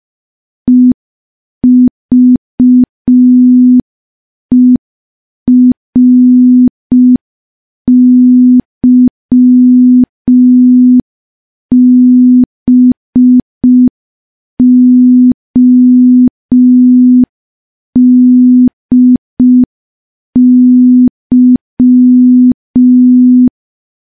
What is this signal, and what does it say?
Morse code, heard as a keyed tone, about 5 wpm, 250 Hz. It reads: EVERYBODY
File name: morse / morse-training-05WPM